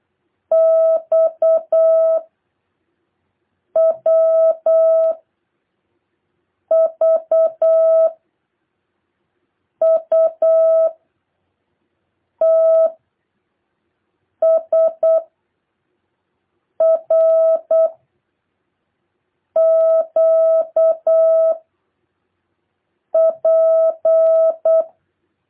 Morse code, heard as a keyed tone, about 8 wpm, 660 Hz. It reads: X W V U T S R Q P